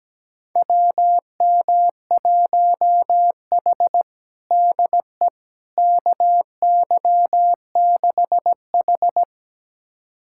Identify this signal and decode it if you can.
Morse code, heard as a keyed tone, about 17 wpm, 700 Hz. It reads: WM1H DE KY6H
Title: USA Callsign DE Callsign - 4 Characters - 17wpm